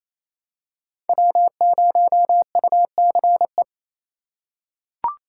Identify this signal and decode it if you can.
Morse code, heard as a keyed tone, about 28 wpm, 700 Hz. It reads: W0UCE
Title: K1USN SST QSO Practice - 28wpm